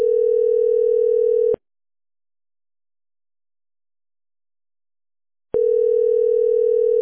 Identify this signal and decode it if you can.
Ringback tone (North America), 440 + 480 Hz.